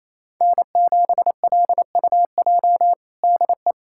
Morse code, heard as a keyed tone, about 28 wpm, 700 Hz. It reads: N7LUJ DE
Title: USA Callsign DE Callsign 28wpm